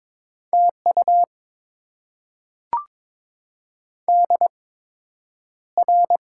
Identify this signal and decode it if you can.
Morse code, heard as a keyed tone, about 22 wpm, spaced as for 9 wpm, 700 Hz. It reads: TU DR
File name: Top 90 QSO Elements - 8x Character Spacing - 22wpm